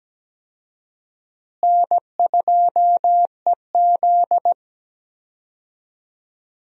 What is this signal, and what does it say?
Morse code, heard as a keyed tone, about 17 wpm, 700 Hz. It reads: N2EZ
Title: USA Call Signs - 4 Characters - Repeated 3x - 17wpm